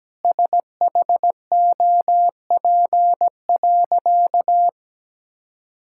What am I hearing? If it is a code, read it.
Morse code, heard as a keyed tone, about 17 wpm, 700 Hz. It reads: SHOP.
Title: Sentences from Top 1600 Words 17wpm